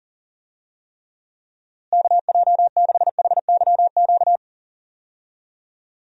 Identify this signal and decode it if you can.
Morse code, heard as a keyed tone, about 40 wpm, 700 Hz. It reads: KJ6HYQ
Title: USA Call Signs - 6 Characters - Repeated 2x - 40wpm